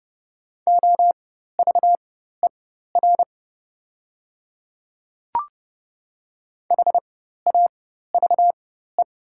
Morse code, heard as a keyed tone, about 30 wpm, spaced as for 14 wpm, 690 Hz. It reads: OVER HAVE
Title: 4x Character Spacing - Top 100 Words - 30wpm